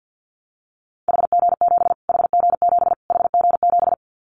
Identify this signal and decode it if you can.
Morse code, heard as a keyed tone, about 50 wpm, 700 Hz. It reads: HG7 HG7 HG7
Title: Callsign Prefixes for Top 25 DXCC - Repeated 3x - 50wpm